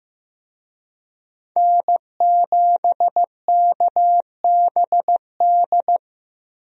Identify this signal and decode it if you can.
Morse code, heard as a keyed tone, about 15 wpm, 700 Hz. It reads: N7KBD
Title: USA Call Signs - 5 Characters - Repeated 3x - 15wpm